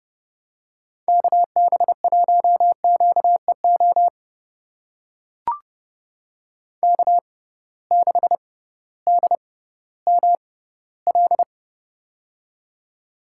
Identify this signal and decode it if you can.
Morse code, heard as a keyed tone, about 30 wpm, spaced as for 16 wpm, 700 Hz. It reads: KB1QEO K6DML
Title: USA Call Signs - 6x Character Spacing - 30wpm